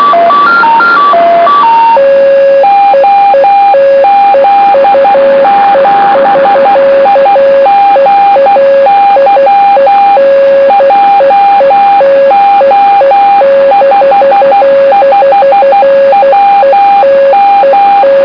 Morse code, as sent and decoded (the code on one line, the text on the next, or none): --- --.. --... .. --. -.-- / .--- --- ..... ..... .-- --
OZ7IGY JO55WM